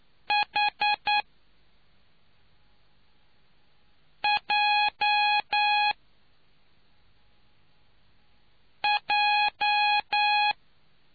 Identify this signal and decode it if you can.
Morse code, heard as a keyed tone, about 9 wpm, spaced as for 3 wpm, 830 Hz. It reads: HJJ